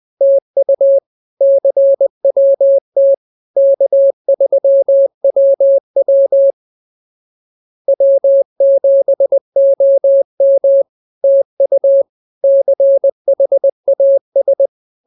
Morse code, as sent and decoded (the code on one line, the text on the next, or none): - ..- / -.-. .-- - / -.- ...-- .-- .-- / .-- --... --- -- / - ..- / -.-. .... .- ...
TU CWT K3WW W7OM TU CHAS